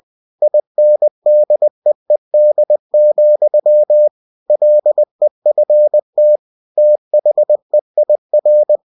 Morse code, heard as a keyed tone, about 20 wpm, 600 Hz. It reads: INDEED, LEFT THEIR
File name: cw-books / winnie-the-pooh / winnie-20wpm-page-0009